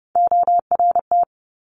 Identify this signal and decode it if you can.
Morse code, heard as a keyed tone, about 30 wpm, 690 Hz. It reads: ORT
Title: Top Three Letter Combinations in Words 30wpm